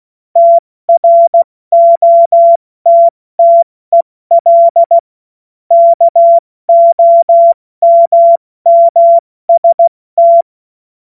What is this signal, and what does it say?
Morse code, heard as a keyed tone, about 16 wpm, 680 Hz. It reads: TROTTEL KOMMST
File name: Morsekurs / Witze / Witz 22. Februar 2022 70 F80